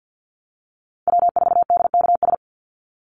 Morse code, heard as a keyed tone, about 49 wpm, 700 Hz. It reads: W4DKS
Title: Call Signs Repeated 3x - 50wpm